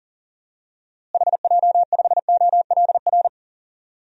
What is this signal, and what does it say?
Morse code, heard as a keyed tone, about 37 wpm, 700 Hz. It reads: HJ5OLR
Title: DXCC-100 Call Signs - 6 Characters - Repeated 3x - 40wpm